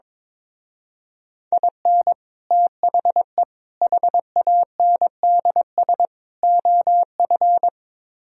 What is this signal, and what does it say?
Morse code, heard as a keyed tone, about 22 wpm, 700 Hz. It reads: IN THE HANDS OF
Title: Speed-Racing - Sets of 4 Words 22wpm